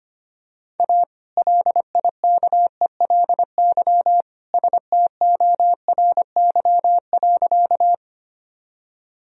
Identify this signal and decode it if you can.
Morse code, heard as a keyed tone, about 25 wpm, 700 Hz. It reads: A LIKELY STORY.
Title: Sentences from Top 1000 Words - Review Part 4 - 25wpm